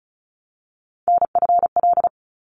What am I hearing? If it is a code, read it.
Morse code, heard as a keyed tone, about 34 wpm, 700 Hz. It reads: NFL
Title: ARRL Sections 35wpm